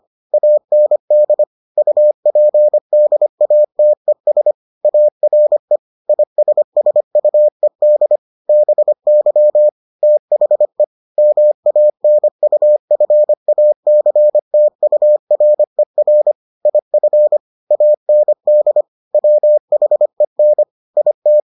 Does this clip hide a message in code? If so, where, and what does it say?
Yes: Morse code from 0.3 to 21.4 s, heard as a keyed tone, about 25 wpm, 600 Hz. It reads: AND UPDATES ARE ISSUED BY THE MANUFACTURER IF AND WHEN IT